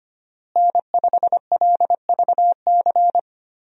Morse code, heard as a keyed tone, about 25 wpm, 700 Hz. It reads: N5LVC
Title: USA Call Signs - 5 Characters - Repeated 2x - 25wpm